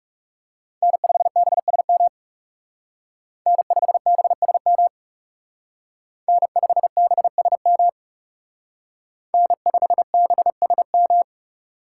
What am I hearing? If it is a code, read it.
Morse code, heard as a keyed tone, about 38 wpm, 700 Hz. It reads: N5BSM N5BSM N5BSM N5BSM